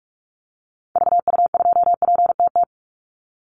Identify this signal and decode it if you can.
Morse code, heard as a keyed tone, about 44 wpm, 700 Hz. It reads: VU2PTT
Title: Call Signs - 45wpm